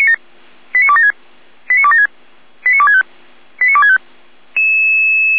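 Five-tone selective call (ZVEI), heard as quick runs of five tones, tones of about 70 ms. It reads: 89276 89276 89265 89265
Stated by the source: ZVEI five-tone selective call received with an SDR.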